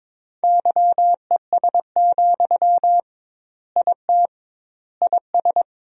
Morse code, heard as a keyed tone, about 21 wpm, 700 Hz. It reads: YES, IT IS
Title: 2x Word-Spacing - Sentences from Top 500 Words - 22wpm